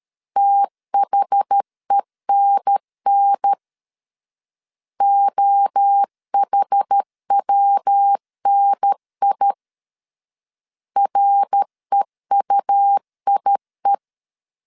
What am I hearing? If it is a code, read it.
Morse code, heard as a keyed tone, about 13 wpm, 800 Hz. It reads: THENN OHWNI REUIE
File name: CW / DET / 1 3 1500C S B 12wpm 2828